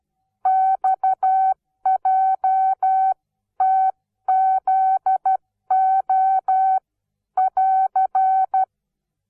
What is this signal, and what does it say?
Morse code, heard as a keyed tone, about 12 wpm, spaced as for 11 wpm, 760 Hz. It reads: XJTZO<AR>
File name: Onlinelehrgang / morsen / DJ4UF Morsen17